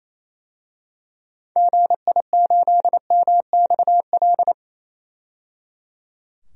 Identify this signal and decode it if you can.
Morse code, heard as a keyed tone, about 28 wpm, 700 Hz. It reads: GI8MXL
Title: DXCC-100 Call Signs - 6 Characters - 28wpm